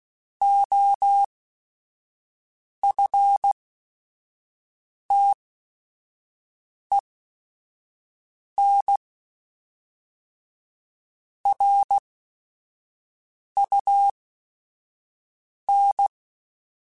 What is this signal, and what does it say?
Morse code, heard as a keyed tone, about 16 wpm, spaced as for 5 wpm, 800 Hz. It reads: OFTEN RUN
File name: QSOs / QSOs-mp3s / kc7sac@5